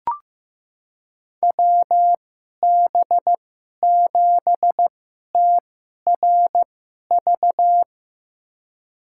Morse code, heard as a keyed tone, about 15 wpm, spaced as for 11 wpm, 700 Hz. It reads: WB7TRV
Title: USA Call Signs - 6 Characters - 2x Character Spacing - 15wpm